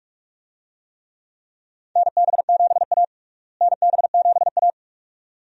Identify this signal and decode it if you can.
Morse code, heard as a keyed tone, about 42 wpm, 700 Hz. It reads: NB7A NB7A
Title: USA Call Signs - 4 Characters - Repeated 2x - 45wpm